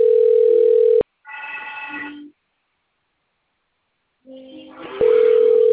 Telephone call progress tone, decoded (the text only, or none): ringback tone